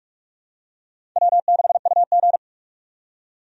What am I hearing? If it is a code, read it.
Morse code, heard as a keyed tone, about 45 wpm, 700 Hz. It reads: W6UG
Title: USA Call Signs - 4 Characters - 45wpm